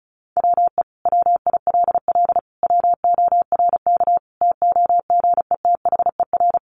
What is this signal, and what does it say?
Morse code, heard as a keyed tone, about 34 wpm, 700 Hz. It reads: WE WILL WORK TOGETHER
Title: Sentences from Top 1100 Words 35wpm